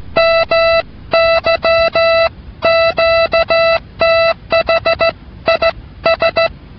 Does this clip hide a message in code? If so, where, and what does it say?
Yes: Morse code from 0.2 to 6.5 s, heard as a keyed tone, about 16 wpm, 660 Hz. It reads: MY QTH IS